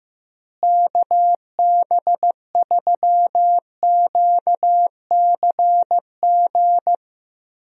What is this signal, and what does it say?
Morse code, heard as a keyed tone, about 15 wpm, 700 Hz. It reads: KB3QCG
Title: USA Call Signs - 6 Characters - Repeated 3x - 15wpm